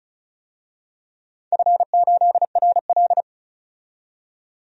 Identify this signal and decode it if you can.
Morse code, heard as a keyed tone, about 34 wpm, 700 Hz. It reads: F8RL
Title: DXCC-100 Call Signs - 4 Characters - 35wpm